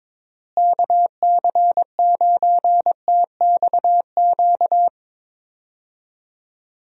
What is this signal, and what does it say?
Morse code, heard as a keyed tone, about 22 wpm, 700 Hz. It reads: KC9TXQ
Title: USA Call Signs - 6 Characters - Repeated 2x - 22wpm